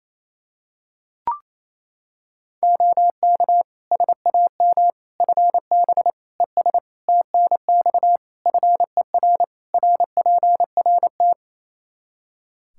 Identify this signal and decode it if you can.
Morse code, heard as a keyed tone, about 28 wpm, 700 Hz. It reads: OK SAM FB ES TNX FER RPRT
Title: Practice QSO - Exchange 2 - Info - 28wpm